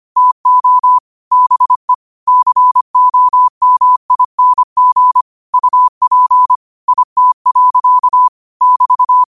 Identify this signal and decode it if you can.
Morse code, heard as a keyed tone, about 25 wpm, 1000 Hz. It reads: TO BE COMING UP IT. <BT>
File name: qotdfiles / 25wpm